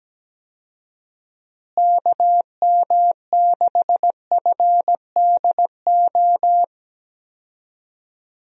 Morse code, heard as a keyed tone, about 17 wpm, 700 Hz. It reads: KM6FDO
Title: USA Call Signs - 6 Characters - Repeated 2x - 17wpm